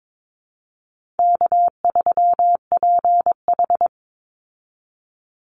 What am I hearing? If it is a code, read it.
Morse code, heard as a keyed tone, about 22 wpm, 700 Hz. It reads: K3PH
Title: Easy Call Signs Repeated 2x - 22wpm